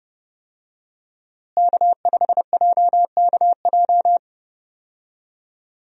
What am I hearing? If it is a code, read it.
Morse code, heard as a keyed tone, about 31 wpm, 700 Hz. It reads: K5JKJ
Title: USA Call Signs - 5 Characters - Repeated 3x - 30wpm